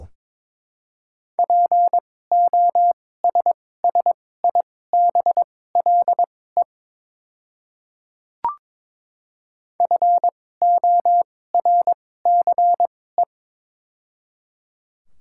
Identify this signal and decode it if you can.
Morse code, heard as a keyed tone, about 22 wpm, spaced as for 16 wpm, 700 Hz. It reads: POSSIBLE FORCE
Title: 2x Character Spacing - Top 500 Words - 22wpm